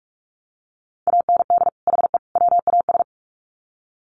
Morse code, heard as a keyed tone, about 44 wpm, 700 Hz. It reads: AND HE WAS
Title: Sets of 3 Words 45wpm